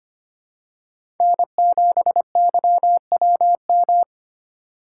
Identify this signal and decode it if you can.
Morse code, heard as a keyed tone, about 25 wpm, 700 Hz. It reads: N7YWM
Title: USA Call Signs - 4x Character Spacing - 25wpm